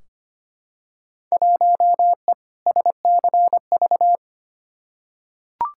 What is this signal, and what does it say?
Morse code, heard as a keyed tone, about 25 wpm, 700 Hz. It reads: 1E SCV
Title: Field Day Exchanges - 25wpm